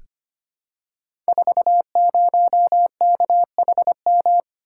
Morse code, heard as a keyed tone, about 25 wpm, 700 Hz. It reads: 40KHM
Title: Code Groups - 5 Letter:Numbers - 25wpm